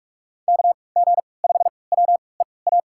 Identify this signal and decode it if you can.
Morse code, heard as a keyed tone, about 46 wpm, spaced as for 26 wpm, 700 Hz. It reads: KG5WEA
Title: USA Call Signs - 6 Characters - 3x Character Spacing - 45wpm